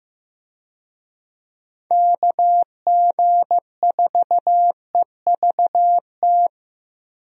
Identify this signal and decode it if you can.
Morse code, heard as a keyed tone, about 15 wpm, 700 Hz. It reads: KG4EVT
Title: USA Call Signs - 6 Characters - 15wpm